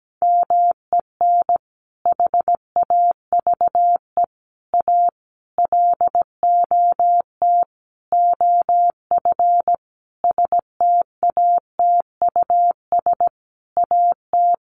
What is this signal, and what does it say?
Morse code, heard as a keyed tone, about 17 wpm, 700 Hz. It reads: MEN HAVE A LOT OF STATUS AT
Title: Sentences from Top 1400 Words 17wpm